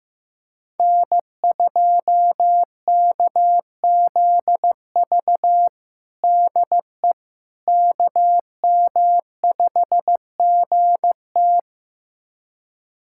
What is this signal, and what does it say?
Morse code, heard as a keyed tone, about 15 wpm, 700 Hz. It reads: N2KZV DE KM5GT
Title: USA Callsign DE Callsign - 5 Characters - 15wpm